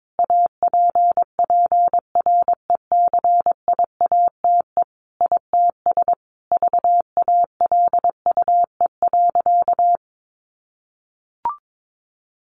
Morse code, heard as a keyed tone, about 22 wpm, 700 Hz. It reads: APPRECIATE ITS VALUE.
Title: Sentences from Top 1300 Words 22wpm